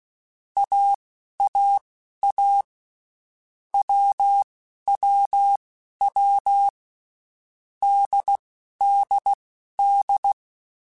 Morse code, heard as a keyed tone, about 16 wpm, spaced as for 11 wpm, 800 Hz. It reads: AAA WWW DDD
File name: Basic Letter Groups By7Groups / AWUNDG / AWUNDG@10